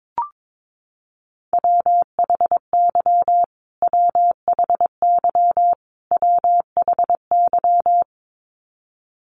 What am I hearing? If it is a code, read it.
Morse code, heard as a keyed tone, about 22 wpm, 700 Hz. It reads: WHY WHY WHY
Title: Top 500 Words Repeated 22wpm